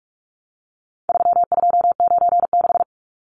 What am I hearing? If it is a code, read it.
Morse code, heard as a keyed tone, about 44 wpm, 700 Hz. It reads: 3296
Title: Numbers - 4-Digit 45wpm